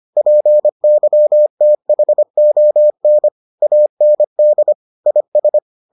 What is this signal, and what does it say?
Morse code, heard as a keyed tone, about 25 wpm, 600 Hz. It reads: PYTHON AND IS